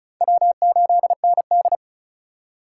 Morse code, heard as a keyed tone, about 36 wpm, 700 Hz. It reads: W8ND
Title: USA Call Signs - 4 Characters - 2x Character Spacing - 35wpm